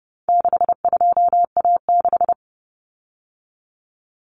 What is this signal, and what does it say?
Morse code, heard as a keyed tone, about 30 wpm, 700 Hz. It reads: 62A6
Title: Common Cut Numbers - 4-Digit 30wpm